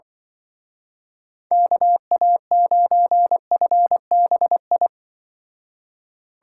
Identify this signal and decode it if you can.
Morse code, heard as a keyed tone, about 24 wpm, 700 Hz. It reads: KA9FBI
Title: Speed Racing - USA 2x3 Call Signs 20wpm